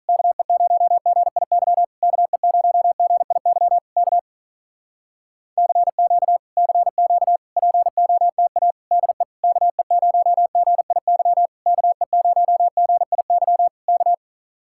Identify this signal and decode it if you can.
Morse code, heard as a keyed tone, about 46 wpm, 700 Hz. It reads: KE0GIY KE0GIY K CQ CQ POTA DE KE0GIY KE0GIY K